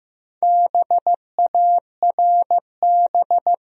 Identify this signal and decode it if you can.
Morse code, heard as a keyed tone, about 15 wpm, 700 Hz. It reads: BARB
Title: USA Common Names - Max 5 Letters - 15wpm